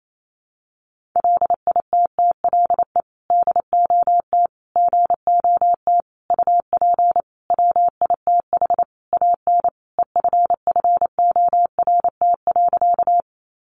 Morse code, heard as a keyed tone, about 28 wpm, 700 Hz. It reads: LITTLE DOT GOT UP WITH AN EFFORT.
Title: Sentences from Top 1500 Words 28wpm